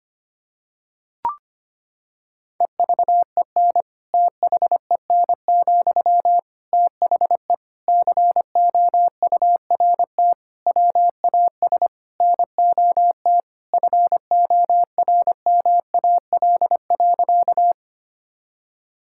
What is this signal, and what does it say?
Morse code, heard as a keyed tone, about 25 wpm, 700 Hz. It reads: EVEN THEN, THE COURT WAS NOT FORMAL.
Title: Sentences from Top 1600 Words 25wpm